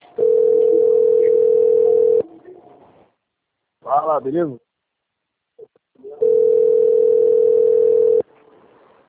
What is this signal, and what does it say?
Ringback tone (North America), 440 + 480 Hz, with speech.
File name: rg-1600-unknown- - - .20868